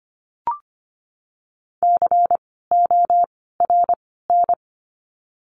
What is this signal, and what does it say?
Morse code, heard as a keyed tone, about 25 wpm, spaced as for 16 wpm, 700 Hz. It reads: CORN
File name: course / 25:10wpm / 094 Lesson 16f - Letter c - Words 25wpm